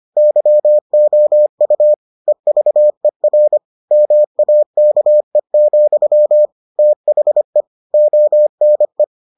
Morse code, heard as a keyed tone, about 25 wpm, 600 Hz. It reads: YOU EVER MAKE, THE ONE